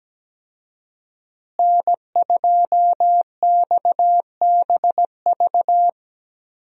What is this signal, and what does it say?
Morse code, heard as a keyed tone, about 17 wpm, 700 Hz. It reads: N2XBV
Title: USA Call Signs - 5 Characters - Repeated 3x - 17wpm